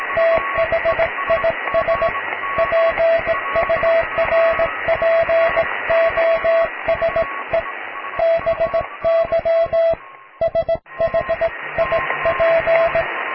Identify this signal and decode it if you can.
Morse code, heard as a keyed tone, about 18 wpm, 660 Hz, with radio noise. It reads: THIS PURPOSE BY SHIP
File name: cw / tekst(b)+QRM